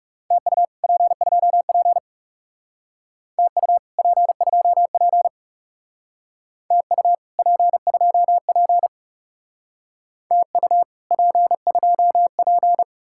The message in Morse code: TU P2P TU P2P TU P2P TU P2P

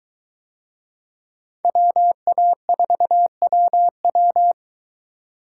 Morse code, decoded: WA4WW

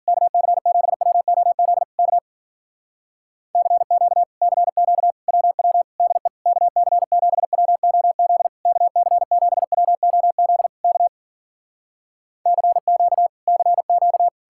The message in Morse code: KC7WOZ K CQ CQ WW DE KC7WOZ KC7WOZ K CQ CQ

